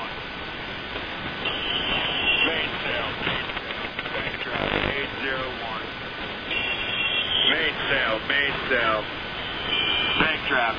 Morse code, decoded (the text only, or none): none